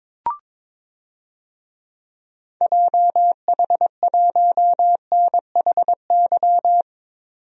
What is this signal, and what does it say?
Morse code, heard as a keyed tone, about 22 wpm, 700 Hz. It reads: JH1NHY
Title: DXCC-100 Call Signs - 6 Characters - 22wpm